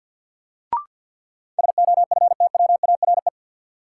Morse code, heard as a keyed tone, about 46 wpm, 700 Hz. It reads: SOFTWARE
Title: Sentences from Top 1000 Words - Review Part 3 - New Word Drill - 50wpm